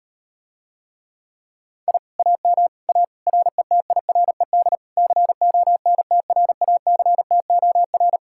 Morse code, heard as a keyed tone, about 38 wpm, 700 Hz. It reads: I AM A RETIRED CONTRACTOR